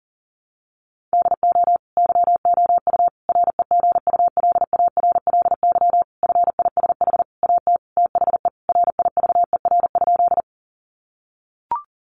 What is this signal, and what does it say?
Morse code, heard as a keyed tone, about 40 wpm, 700 Hz. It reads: DO YOU REGULARLY FISH AT THE RIVER?
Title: Sentences from Top 1400 Words 40wpm